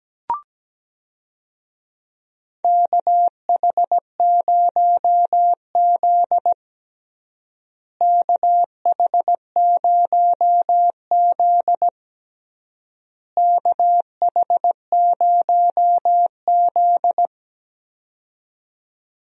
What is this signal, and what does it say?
Morse code, heard as a keyed tone, about 17 wpm, 700 Hz. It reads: KH0Z KH0Z KH0Z
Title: USA Call Signs - 4 Characters - Repeated 3x - 17wpm